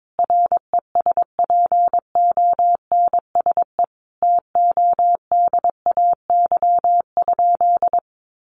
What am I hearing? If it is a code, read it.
Morse code, heard as a keyed tone, about 22 wpm, 700 Hz. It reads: RESPONSE TODAY?
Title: Sentences from Top 1400 Words 22wpm